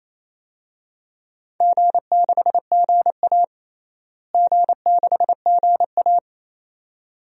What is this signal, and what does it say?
Morse code, heard as a keyed tone, about 28 wpm, 700 Hz. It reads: G6GA G6GA
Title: DXCC-100 Call Signs - 4 Characters - Repeated 2x - 28wpm